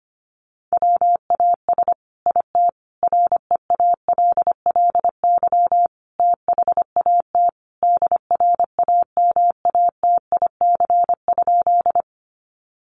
Morse code, heard as a keyed tone, about 25 wpm, 700 Hz. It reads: WAS IT REALLY THAT DRAMATIC?
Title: Sentences from Top 1300 Words 25wpm